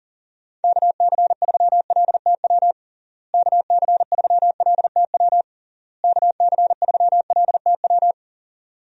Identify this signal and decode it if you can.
Morse code, heard as a keyed tone, about 40 wpm, 700 Hz. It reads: KC3LTW KC3LTW KC3LTW
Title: USA Call Signs - 6 Characters - Repeated 3x - 40wpm